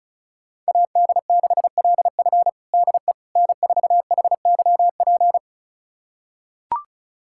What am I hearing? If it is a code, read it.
Morse code, heard as a keyed tone, about 36 wpm, 700 Hz. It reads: AD6LF DE N4HYP
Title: USA Callsign DE Callsign - 5 Characters - 35wpm